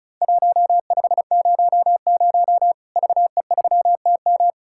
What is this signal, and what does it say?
Morse code, heard as a keyed tone, about 36 wpm, 700 Hz. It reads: 1500 VE3TM
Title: Worked All Europe DX Contest - QTC Exchange - 35wpm